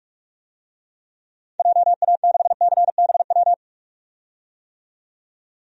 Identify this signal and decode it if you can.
Morse code, heard as a keyed tone, about 47 wpm, 700 Hz. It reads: JA6CBW
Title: DXCC-100 Call Signs - 6 Characters - 45wpm